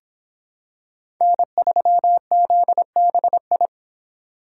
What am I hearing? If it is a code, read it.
Morse code, heard as a keyed tone, about 26 wpm, 700 Hz. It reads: N3ZBI